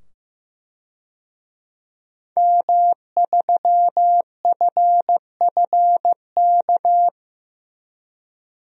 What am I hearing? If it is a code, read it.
Morse code, heard as a keyed tone, about 15 wpm, 700 Hz. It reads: M3FFK
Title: DXCC-100 Call Signs - 5 Characters - 15wpm